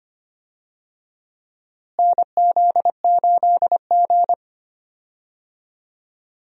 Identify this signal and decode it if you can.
Morse code, heard as a keyed tone, about 25 wpm, 700 Hz. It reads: NZ8G
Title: USA Call Signs - 4 Characters - Repeated 3x - 25wpm